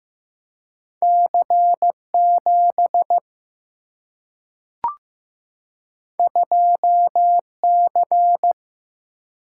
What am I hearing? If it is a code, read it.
Morse code, heard as a keyed tone, about 15 wpm, 700 Hz. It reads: C7 2C
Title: Single-Letter Single-Number 15wpm